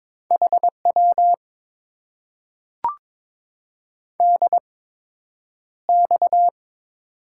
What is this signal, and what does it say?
Morse code, heard as a keyed tone, about 22 wpm, spaced as for 9 wpm, 700 Hz. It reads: HW DX
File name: Top 90 QSO Elements - 8x Character Spacing - 22wpm